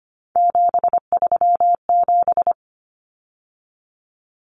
Morse code, heard as a keyed tone, about 25 wpm, 700 Hz. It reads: 737
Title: Numbers - 3-Digit 25wpm